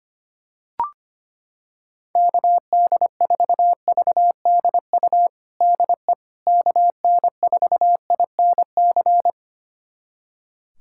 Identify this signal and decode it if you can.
Morse code, heard as a keyed tone, about 25 wpm, 700 Hz. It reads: KD4VDU DE KN4INC
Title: USA Callsign DE Callsign - 6 Characters - 25wpm